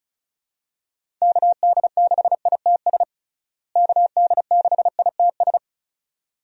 Morse code, heard as a keyed tone, about 36 wpm, 700 Hz. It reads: KD6ITS KD6ITS
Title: USA Call Signs - 6 Characters - Repeated 2x - 35wpm